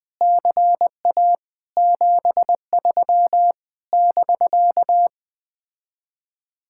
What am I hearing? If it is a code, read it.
Morse code, heard as a keyed tone, about 20 wpm, 700 Hz. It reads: CA 73 <BK>
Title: Speed-Racing Common QSO Phrases Encore - 17wpm